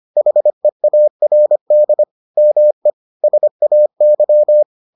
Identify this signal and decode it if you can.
Morse code, heard as a keyed tone, about 25 wpm, 600 Hz. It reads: HEARD ME SAY